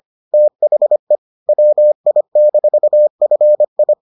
Morse code, heard as a keyed tone, about 25 wpm, 600 Hz. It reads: THE WI-FI